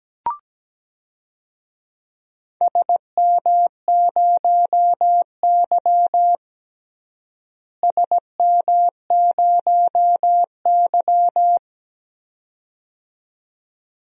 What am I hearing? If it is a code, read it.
Morse code, heard as a keyed tone, about 17 wpm, 700 Hz. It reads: SM0Y SM0Y
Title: DXCC-100 Call Signs - 4 Characters - Repeated 2x - 17wpm